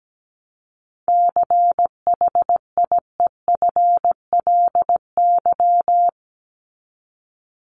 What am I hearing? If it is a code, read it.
Morse code, heard as a keyed tone, about 17 wpm, 700 Hz. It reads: CHIEFLY
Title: LY Suffix - Top 100 Words with Base Words 17wpm